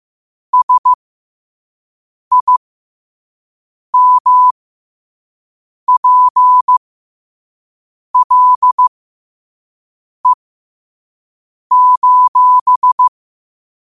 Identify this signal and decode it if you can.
Morse code, heard as a keyed tone, about 15 wpm, spaced as for 5 wpm, 1000 Hz. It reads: SIMPLE:
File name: qotdfiles / 5wpm